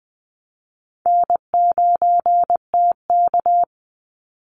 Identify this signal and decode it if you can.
Morse code, heard as a keyed tone, about 19 wpm, 700 Hz. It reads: N9TK
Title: Easy Call Signs - 20wpm